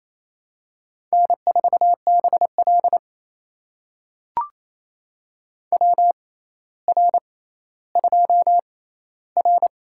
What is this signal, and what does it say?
Morse code, heard as a keyed tone, about 28 wpm, spaced as for 14 wpm, 700 Hz. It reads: N4BL WR2R
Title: USA Call Signs - 4 Characters - 6x Character Spacing - 28wpm